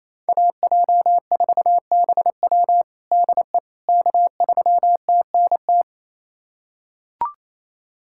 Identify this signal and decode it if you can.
Morse code, heard as a keyed tone, about 28 wpm, 700 Hz. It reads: AJ4BW DE K3TNT
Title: USA Callsign DE Callsign - 5 Characters - 28wpm